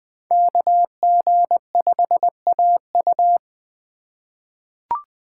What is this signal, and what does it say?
Morse code, heard as a keyed tone, about 20 wpm, 700 Hz. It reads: KG5AU
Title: USA Call Signs - 5 Characters - 3x Character Spacing - 20wpm